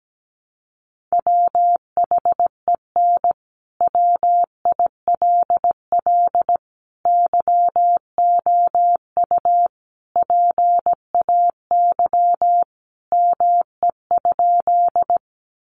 Morse code, heard as a keyed tone, about 17 wpm, 700 Hz. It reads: WHEN WILL YOU PAY ME?
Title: Sentences from Top 700 Words 17wpm